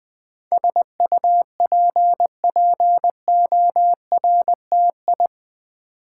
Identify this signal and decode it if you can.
Morse code, heard as a keyed tone, about 20 wpm, 700 Hz. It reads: SUPPORTI
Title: Build A Word - 4 Letters At A Time - 10-Letter Words - 20wpm